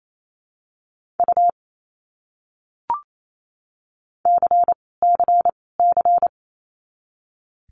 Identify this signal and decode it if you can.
Morse code, heard as a keyed tone, about 28 wpm, 700 Hz. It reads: U C C C
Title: Abbreviations Repeated 3x - 28wpm